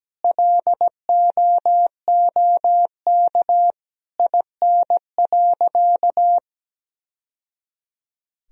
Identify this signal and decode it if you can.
Morse code, heard as a keyed tone, about 17 wpm, 700 Hz. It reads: LOOK IN.
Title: Sentences from Top 1000 Words - Review Part 2 - 17wpm